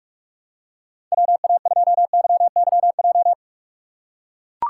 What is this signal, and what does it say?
Morse code, heard as a keyed tone, about 44 wpm, 700 Hz. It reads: WA2YYJ
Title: USA Call Signs - 6 Characters - 6x Character Spacing - 45wpm